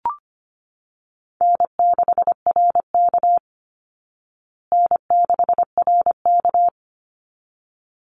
Morse code, heard as a keyed tone, about 25 wpm, 700 Hz. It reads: N6RK N6RK
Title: Easy Call Signs Repeated 2x - 25wpm